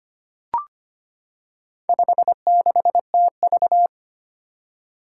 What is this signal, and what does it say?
Morse code, heard as a keyed tone, about 25 wpm, 700 Hz. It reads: 56TV
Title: Four Challenging Letter-Number Combinations 25wpm